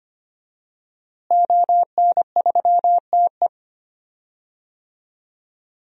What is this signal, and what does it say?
Morse code, heard as a keyed tone, about 25 wpm, 700 Hz. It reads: ON3TE
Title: DXCC-100 Call Signs - 5 Characters - Repeated 2x - 25wpm